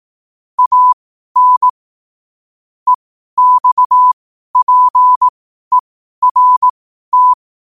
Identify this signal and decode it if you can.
Morse code, heard as a keyed tone, about 18 wpm, spaced as for 12 wpm, 1000 Hz. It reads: AN EXPERT
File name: qotdfiles / 12wpm